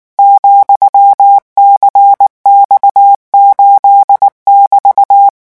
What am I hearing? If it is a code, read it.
Morse code, heard as a keyed tone, about 19 wpm, 800 Hz. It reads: ,CX8<BT>
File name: morse mp3 / revision15-vitesse 18 mots